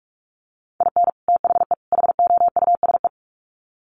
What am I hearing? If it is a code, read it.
Morse code, heard as a keyed tone, about 44 wpm, 700 Hz. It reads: IN THE HOUSE
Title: Sets of 3 Words 45wpm